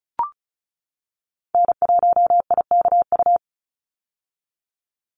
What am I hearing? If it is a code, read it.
Morse code, heard as a keyed tone, about 34 wpm, 700 Hz. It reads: N1IKU
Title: USA Call Signs 35wpm